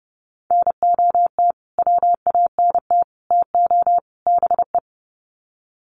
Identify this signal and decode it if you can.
Morse code, heard as a keyed tone, about 30 wpm, 700 Hz. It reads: NOT WANT TO BE
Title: Sets of 4 Words 30wpm